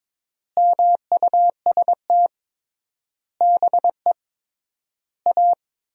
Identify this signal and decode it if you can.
Morse code, heard as a keyed tone, about 22 wpm, 700 Hz. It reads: MUST BE A
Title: 3x Word-Spacing - Sentences from Top 500 Words - 22wpm